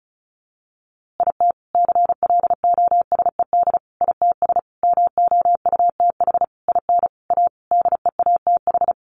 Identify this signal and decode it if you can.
Morse code, heard as a keyed tone, about 34 wpm, 700 Hz. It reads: IT CLOSED ITS MOUTH IN A DEATH
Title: A Sound Of Thunder 35wpm